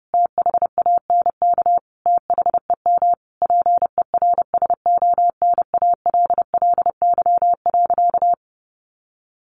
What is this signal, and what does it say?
Morse code, heard as a keyed tone, about 30 wpm, 700 Hz. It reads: THANK THEM PERSONALLY.
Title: Sentences from Top 1400 Words 30wpm